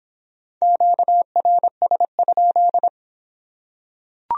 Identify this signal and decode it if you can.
Morse code, heard as a keyed tone, about 24 wpm, 700 Hz. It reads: QRS?